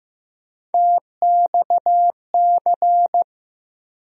Morse code, heard as a keyed tone, about 15 wpm, 700 Hz. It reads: TXC
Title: USA Call Sign Postfixes - 3 Letters 15wpm